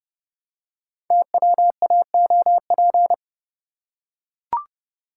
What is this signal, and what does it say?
Morse code, heard as a keyed tone, about 30 wpm, 700 Hz. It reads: TWAOP
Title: Code Groups - 5 Letters - 30wpm